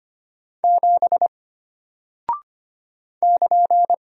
Morse code, heard as a keyed tone, about 25 wpm, 700 Hz. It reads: 7 <KN>
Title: Single Character (Letter:Number:Punctuation:Prosign) Alt 25wpm